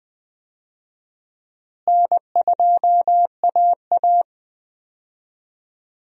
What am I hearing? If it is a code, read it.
Morse code, heard as a keyed tone, about 21 wpm, 700 Hz. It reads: N2AA
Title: USA Call Signs - 4 Characters - Repeated 3x - 20wpm